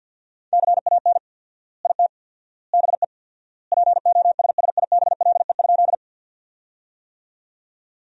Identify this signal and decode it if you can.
Morse code, heard as a keyed tone, about 52 wpm, 700 Hz. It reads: CAN IT BE POSSIBLE?
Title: 4x Word-Spacing - Sentences from Top 500 Words - 50wpm